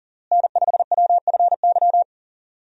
Morse code, heard as a keyed tone, about 40 wpm, 700 Hz. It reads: N5WFY
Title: USA Call Signs - 5 Characters - Repeated 3x - 40wpm